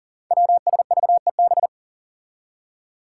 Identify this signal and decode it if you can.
Morse code, heard as a keyed tone, about 39 wpm, 700 Hz. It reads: WSVEB
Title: Five Challenging Letter-Number Combinations 40wpm